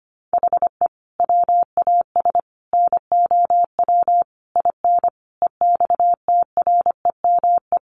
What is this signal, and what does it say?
Morse code, heard as a keyed tone, about 25 wpm, 700 Hz. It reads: HE WAS NOW IN EXTREME
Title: Sentences from Top 1300 Words 25wpm